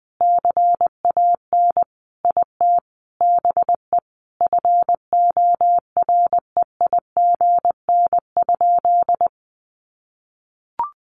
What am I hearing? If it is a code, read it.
Morse code, heard as a keyed tone, about 20 wpm, 700 Hz. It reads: CAN IT BE FOREIGN?
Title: Sentences from Top 1400 Words 20wpm